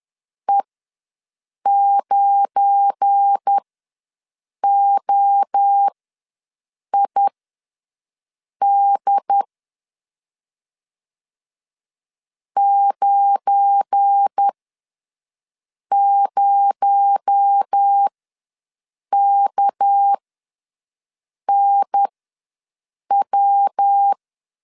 Morse code, heard as a keyed tone, about 10 wpm, spaced as for 5 wpm, 800 Hz. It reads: E9OID 90KNW
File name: LEZIONI / 7 GF9Z / 7 GF9 ws 10wpm 4034